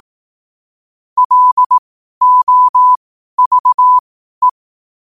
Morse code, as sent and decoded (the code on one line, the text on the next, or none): .-.. --- ...- .
LOVE